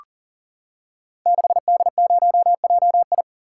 Morse code, heard as a keyed tone, about 39 wpm, 700 Hz. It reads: 6D0JI
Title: Code Groups - 5 Letter:Numbers - 40wpm